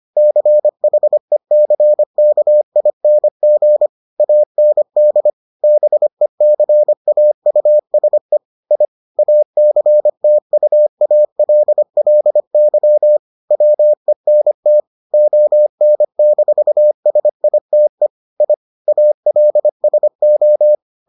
Morse code, heard as a keyed tone, about 25 wpm, 600 Hz. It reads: CHECKING AND BECAUSE I ACTUALLY WENT ON-SITE I ALSO